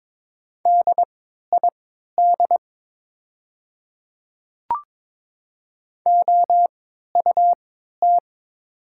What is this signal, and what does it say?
Morse code, heard as a keyed tone, about 22 wpm, spaced as for 12 wpm, 700 Hz. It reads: DID OUT
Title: 3x Character Spacing - Top 100 Words - 22wpm